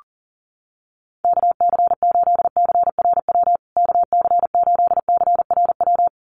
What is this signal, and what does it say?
Morse code, heard as a keyed tone, about 40 wpm, 700 Hz. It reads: KC8CRW KC8CRW
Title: USA Call Signs Repeated 2x 40wpm